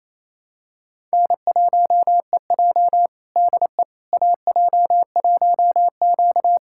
Morse code, heard as a keyed tone, about 28 wpm, 700 Hz. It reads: N1EJ DE AJ1Q
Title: USA Callsign DE Callsign - 4 Characters - 28wpm